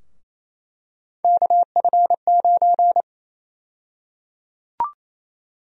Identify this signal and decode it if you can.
Morse code, heard as a keyed tone, about 27 wpm, 700 Hz. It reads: KF9
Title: USA Call Sign Prefixes - Repeated 2x - 28wpm